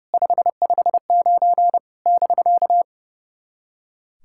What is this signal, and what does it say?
Morse code, heard as a keyed tone, about 30 wpm, 700 Hz. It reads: 559 <BK>
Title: SOTA QSO Practice 30wpm